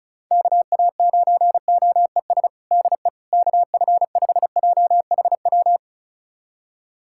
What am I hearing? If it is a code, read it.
Morse code, heard as a keyed tone, about 34 wpm, 690 Hz. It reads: KA9OES DE KF5JHW
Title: USA Callsign DE Callsign 35wpm